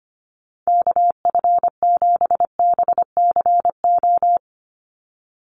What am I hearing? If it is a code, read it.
Morse code, heard as a keyed tone, about 25 wpm, 700 Hz. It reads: KF7BCO
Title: USA Call Signs 25wpm